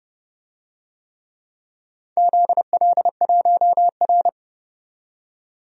Morse code, heard as a keyed tone, about 30 wpm, 700 Hz. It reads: ZL1R
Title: DXCC-100 Call Signs - 4 Characters - 30wpm